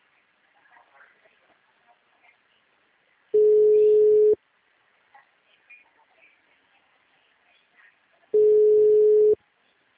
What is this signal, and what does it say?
Ringback tone (Europe, 425 Hz).